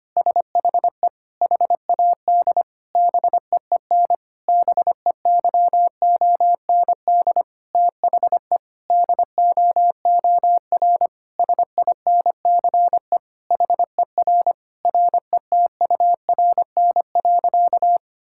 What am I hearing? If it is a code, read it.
Morse code, heard as a keyed tone, about 25 wpm, 700 Hz. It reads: SHE HAD BEEN BEYOND THE DOOR SINCE HER RETURN.
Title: Sentences from Top 1000 Words - Review Part 6 - 25wpm